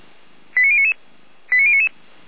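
Five-tone selective call (ZVEI), heard as quick runs of five tones, tones of about 70 ms. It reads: 89099 89099
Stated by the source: Five-tone ZVEI selective call received on an SDR, fire brigade alerting.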